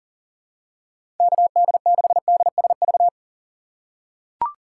Morse code, heard as a keyed tone, about 40 wpm, 700 Hz. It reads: KD6DSV
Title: USA Call Signs - 6 Characters - 3x Character Spacing - 40wpm